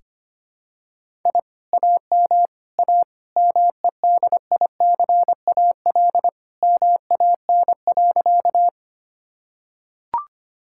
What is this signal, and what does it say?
Morse code, heard as a keyed tone, about 25 wpm, 700 Hz. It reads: I AM A MEDICAL MAN.
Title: Sentences from Top 1000 Words - Review Part 5 - 25wpm